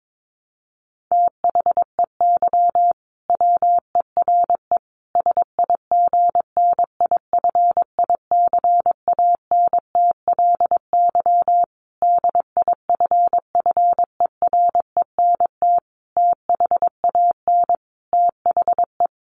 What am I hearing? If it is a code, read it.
Morse code, heard as a keyed tone, about 22 wpm, 700 Hz. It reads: THEY WERE SIGNIFICANTLY DIFFERENT THAN THE
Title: Sentences from Top 1500 Words 22wpm